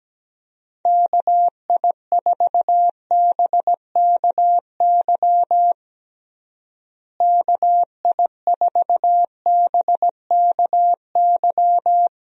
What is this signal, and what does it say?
Morse code, heard as a keyed tone, about 17 wpm, 700 Hz. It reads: KI4BKY KI4BKY
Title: USA Call Signs - 6 Characters - Repeated 2x - 17wpm